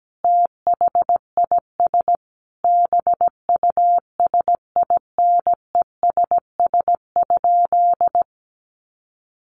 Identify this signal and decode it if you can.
Morse code, heard as a keyed tone, about 17 wpm, 700 Hz. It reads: THIS BUSINESS?
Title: Sentences from Top 600 Words 17wpm